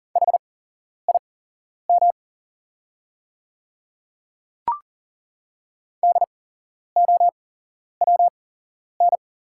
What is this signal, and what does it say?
Morse code, heard as a keyed tone, about 40 wpm, spaced as for 11 wpm, 700 Hz. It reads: HIM DOWN
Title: 8x Character Spacing - Top 100 Words - 40wpm